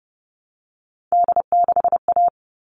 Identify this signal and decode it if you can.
Morse code, heard as a keyed tone, about 30 wpm, 700 Hz. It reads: D6A
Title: All Cut Numbers - 3-Digit 30wpm